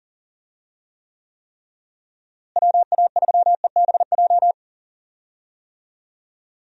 Morse code, heard as a keyed tone, about 40 wpm, 700 Hz. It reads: WA3EBJ
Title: USA Call Signs - 6 Characters - Repeated 3x - 40wpm